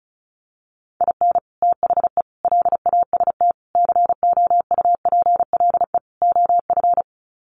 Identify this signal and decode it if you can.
Morse code, heard as a keyed tone, about 34 wpm, 700 Hz. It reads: IN THE LAST COUPLE OF
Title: Sets of 5 Words 35wpm